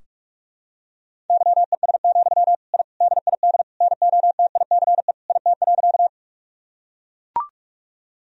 Morse code, heard as a keyed tone, about 41 wpm, 700 Hz. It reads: YES, I DID NOTICE IT.
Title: Sentences from Top 1000 Words - Review Part 2 - 45wpm